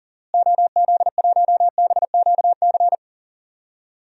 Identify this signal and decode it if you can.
Morse code, heard as a keyed tone, about 40 wpm, 700 Hz. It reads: OZ1BQC
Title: DXCC-100 Call Signs - 6 Characters - Repeated 2x - 40wpm